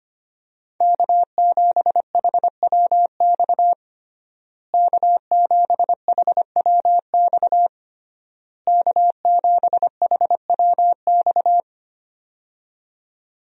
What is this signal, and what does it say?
Morse code, heard as a keyed tone, about 25 wpm, 700 Hz. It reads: K7HWX K7HWX K7HWX
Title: USA Call Signs - 5 Characters - Repeated 3x - 25wpm